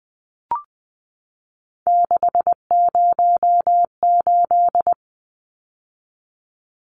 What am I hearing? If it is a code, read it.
Morse code, heard as a keyed tone, about 20 wpm, 700 Hz. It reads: 608
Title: Numbers - 3-Digit 20wpm